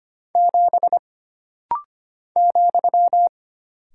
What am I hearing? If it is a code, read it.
Morse code, heard as a keyed tone, about 25 wpm, 700 Hz. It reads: 7 ,
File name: Single Character (Letter:Number:Punctuation:Prosign) - Rapid-Fire - 25wpm